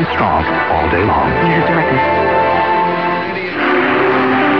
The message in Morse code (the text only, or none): none